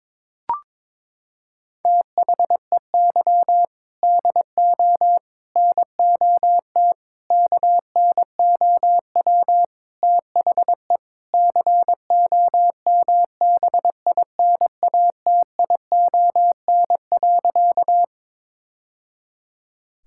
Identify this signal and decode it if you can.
Morse code, heard as a keyed tone, about 21 wpm, 700 Hz. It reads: THEY DO NOT KNOW THE COMBINATION.
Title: Sentences from Top 1000 Words - Review Part 6 - 22wpm